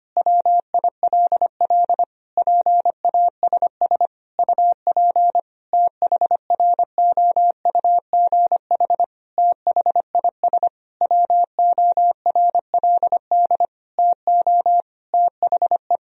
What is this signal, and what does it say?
Morse code, heard as a keyed tone, about 25 wpm, 700 Hz. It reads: WILL PASS UP THROUGH THIS WORLD TO THE
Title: Sentences from Top 1000 Words - Review Part 2 - 25wpm